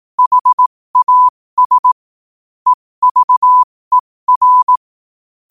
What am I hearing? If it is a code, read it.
Morse code, heard as a keyed tone, about 18 wpm, spaced as for 15 wpm, 1000 Hz. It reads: HAS EVER
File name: qotdfiles / 15wpm